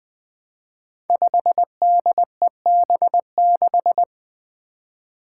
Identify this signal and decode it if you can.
Morse code, heard as a keyed tone, about 20 wpm, 700 Hz. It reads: 5DEB6
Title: Five Challenging Letter-Number Combinations 20wpm